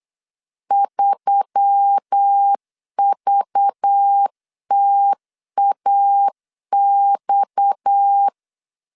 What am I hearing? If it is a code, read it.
Morse code, heard as a keyed tone, about 8 wpm, 800 Hz. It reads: 3VTAX
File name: CW / DET / 1 9 1500C S C 8wpm 5653